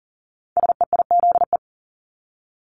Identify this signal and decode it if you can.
Morse code, heard as a keyed tone, about 39 wpm, 700 Hz. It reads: SEIZE
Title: Multi-letter Phonemes Repeated 3x 40wpm